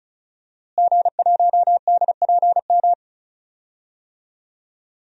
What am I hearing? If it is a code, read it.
Morse code, heard as a keyed tone, about 36 wpm, 700 Hz. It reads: G1DPM
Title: DXCC-100 Call Signs - 5 Characters - Repeated 3x - 35wpm